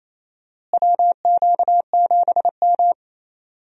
Morse code, heard as a keyed tone, about 28 wpm, 700 Hz. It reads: WQ7M
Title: USA Call Signs - 4 Characters - Repeated 3x - 28wpm